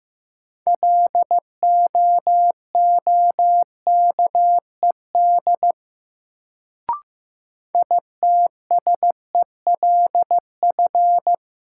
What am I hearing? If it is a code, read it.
Morse code, heard as a keyed tone, about 15 wpm, 700 Hz. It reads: LOOKED ITSELF
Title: Sentences from Top 800 Words - New Word Drill - 15wpm